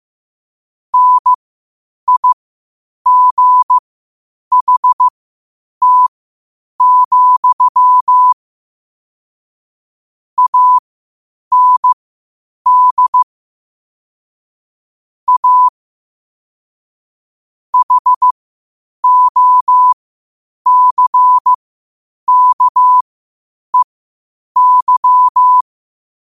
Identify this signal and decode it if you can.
Morse code, heard as a keyed tone, about 15 wpm, spaced as for 8 wpm, 1000 Hz. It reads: NIGHT, AND A HOCKEY
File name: qotdfiles / 8wpm